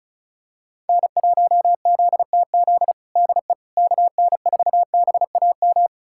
Morse code, heard as a keyed tone, about 36 wpm, 700 Hz. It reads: N1ZTZ DE KN4BAM
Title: USA Callsign DE Callsign 35wpm